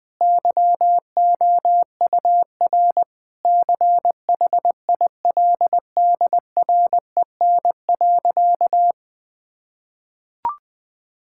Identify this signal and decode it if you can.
Morse code, heard as a keyed tone, about 21 wpm, 700 Hz. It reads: YOUR CHILDREN.
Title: Sentences from Top 1000 Words - Review Part 6 - 20wpm